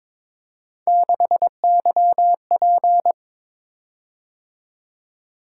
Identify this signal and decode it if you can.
Morse code, heard as a keyed tone, about 22 wpm, 700 Hz. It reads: 6YP
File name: practice / 22wpm / Three Letter-Number Combinations - 22wpm